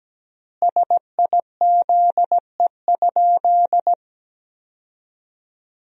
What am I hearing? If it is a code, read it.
Morse code, heard as a keyed tone, about 17 wpm, 700 Hz. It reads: SIZE?
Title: 3x Word-Spacing - Sentences from Top 500 Words - 17wpm